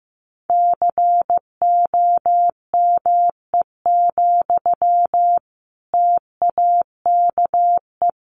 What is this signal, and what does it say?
Morse code, heard as a keyed tone, about 15 wpm, 700 Hz. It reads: COME, TAKE
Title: Sentences from Top 1200 Words 15wpm